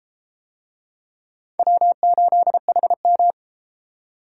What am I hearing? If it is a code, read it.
Morse code, heard as a keyed tone, about 33 wpm, 700 Hz. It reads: W8HM